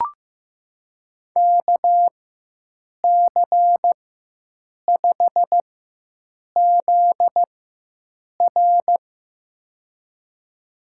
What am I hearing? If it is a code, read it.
Morse code, heard as a keyed tone, about 15 wpm, spaced as for 7 wpm, 700 Hz. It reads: KC5ZR
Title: USA Call Signs - 5 Characters - 4x Character Spacing - 15wpm